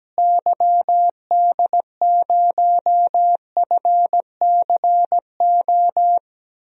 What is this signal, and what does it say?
Morse code, heard as a keyed tone, about 17 wpm, 700 Hz. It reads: YD0FCO
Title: DXCC-100 Call Signs - 6 Characters - Repeated 2x - 17wpm